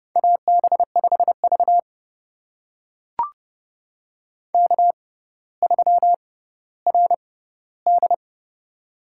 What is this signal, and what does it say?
Morse code, heard as a keyed tone, about 30 wpm, spaced as for 15 wpm, 700 Hz. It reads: AB5V K3RD
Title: USA Call Signs - 4 Characters - 6x Character Spacing - 30wpm